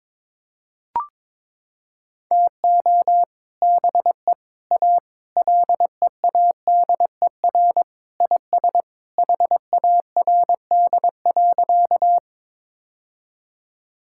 Morse code, heard as a keyed tone, about 21 wpm, 700 Hz. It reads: TO BE A LEADER IS HARD.
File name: practice / 22wpm / Sentences with 3-Word Phrases - 22wpm